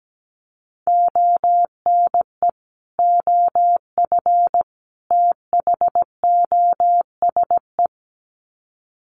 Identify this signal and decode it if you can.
Morse code, heard as a keyed tone, about 17 wpm, 700 Hz. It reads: ONE OF THOSE
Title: Sets of 3 Words 17wpm